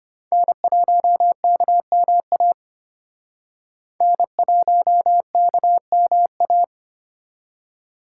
Morse code, heard as a keyed tone, about 33 wpm, 700 Hz. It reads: N1KMA N1KMA